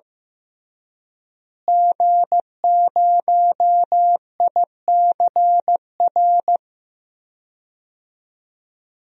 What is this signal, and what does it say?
Morse code, heard as a keyed tone, about 15 wpm, 700 Hz. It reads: G0ICR
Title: DXCC-100 Call Signs - 5 Characters - Repeated 3x - 15wpm